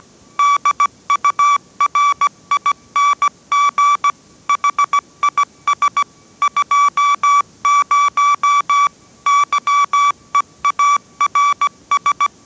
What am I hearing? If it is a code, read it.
Morse code, heard as a keyed tone, about 17 wpm, 1150 Hz. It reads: DURING HIS 20 YEARS